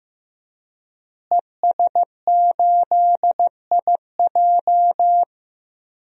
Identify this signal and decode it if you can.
Morse code, heard as a keyed tone, about 16 wpm, 700 Hz. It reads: ES8IJ
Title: DXCC-100 Call Signs - 5 Characters - Repeated 2x - 15wpm